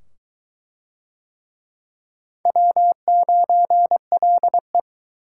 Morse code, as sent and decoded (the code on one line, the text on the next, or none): .-- ----. .-.. .
W9LE